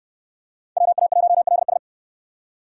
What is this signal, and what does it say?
Morse code, heard as a keyed tone, about 63 wpm, 700 Hz. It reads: WA1LS